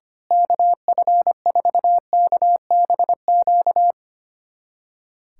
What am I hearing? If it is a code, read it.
Morse code, heard as a keyed tone, about 25 wpm, 700 Hz. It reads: KF4KBQ
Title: USA Callsign DE Callsign - 6 Characters - 25wpm